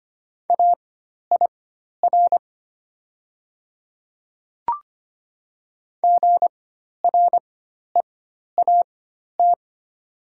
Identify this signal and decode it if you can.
Morse code, heard as a keyed tone, about 25 wpm, spaced as for 12 wpm, 700 Hz. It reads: AIR GREAT
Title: 4x Character Spacing - Top 300 Words - 25wpm